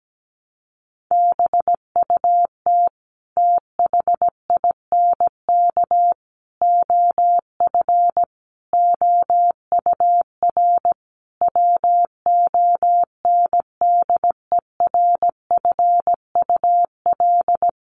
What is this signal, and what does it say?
Morse code, heard as a keyed tone, about 17 wpm, 700 Hz. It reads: BUT THINK OF OUR WONDERFUL
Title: Sentences from Top 1100 Words 17wpm